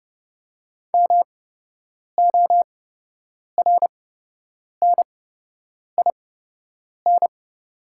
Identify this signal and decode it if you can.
Morse code, heard as a keyed tone, about 30 wpm, spaced as for 8 wpm, 700 Hz. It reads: MORNIN
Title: 8x Character Spacing - Top 500 Words - 30wpm